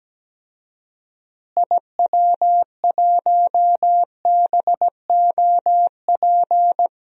Morse code, heard as a keyed tone, about 17 wpm, 700 Hz. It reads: IW1BOP
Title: DXCC-100 Call Signs - 6 Characters - Repeated 2x - 17wpm